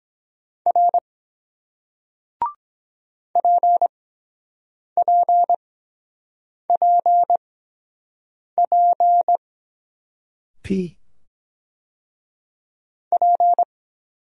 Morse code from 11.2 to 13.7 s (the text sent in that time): P